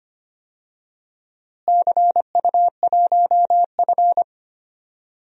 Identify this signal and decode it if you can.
Morse code, heard as a keyed tone, about 25 wpm, 700 Hz. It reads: CU1F
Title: DXCC-100 Call Signs - 4 Characters - Repeated 2x - 25wpm